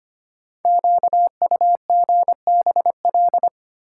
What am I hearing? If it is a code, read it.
Morse code, heard as a keyed tone, about 25 wpm, 700 Hz. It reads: QUGBL
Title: Code Groups - 5 Letters - 25wpm